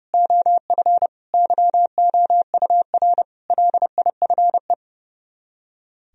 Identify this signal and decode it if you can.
Morse code, heard as a keyed tone, about 28 wpm, 700 Hz. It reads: OF YOUR LIFE
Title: Speed-Racing - Sets of 5 Words 30wpm